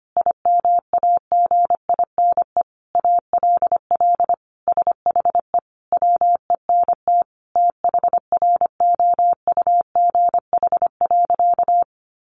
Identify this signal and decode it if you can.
Morse code, heard as a keyed tone, about 25 wpm, 700 Hz. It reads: IMAGINE ALL SHE WENT THROUGH.
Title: Sentences from Top 1400 Words 25wpm